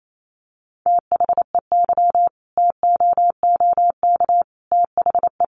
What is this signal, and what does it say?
Morse code, heard as a keyed tone, about 28 wpm, 700 Hz. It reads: THEY TOOK THE
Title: Sentences from Top 1200 Words 28wpm